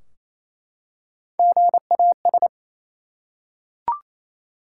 Morse code, heard as a keyed tone, about 28 wpm, 700 Hz. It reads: GAS
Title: Three Letter Words - 28wpm